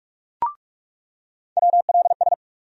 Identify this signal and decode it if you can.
Morse code, heard as a keyed tone, about 46 wpm, 700 Hz. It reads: WLS
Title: USA Call Sign Postfixes - 3 Letters 45wpm